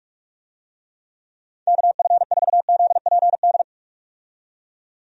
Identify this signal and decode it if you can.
Morse code, heard as a keyed tone, about 46 wpm, 700 Hz. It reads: KF4ZPD
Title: USA Call Signs - 6 Characters - 45wpm